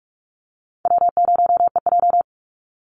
Morse code, heard as a keyed tone, about 44 wpm, 700 Hz. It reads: W0EJ
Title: Easy Call Signs Repeated 2x - 45wpm